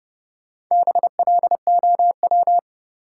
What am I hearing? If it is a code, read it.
Morse code, heard as a keyed tone, about 31 wpm, 700 Hz. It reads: BLOW
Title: Four Letter Words - 30wpm